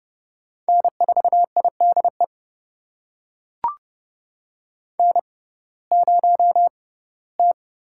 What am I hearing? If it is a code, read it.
Morse code, heard as a keyed tone, about 32 wpm, spaced as for 19 wpm, 700 Hz. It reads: N4IDE N0T
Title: USA Call Signs - 5 Characters - 6x Character Spacing - 30wpm